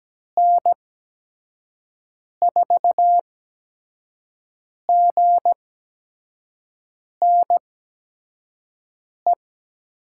Morse code, heard as a keyed tone, about 17 wpm, spaced as for 5 wpm, 700 Hz. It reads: N4GNE
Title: USA Call Signs - 8x Character Spacing - 17wpm